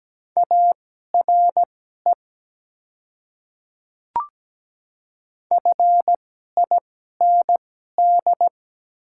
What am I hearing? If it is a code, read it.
Morse code, heard as a keyed tone, about 17 wpm, spaced as for 12 wpm, 700 Hz. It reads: ARE FIND
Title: 2x Character Spacing - Top 200 Words - 17wpm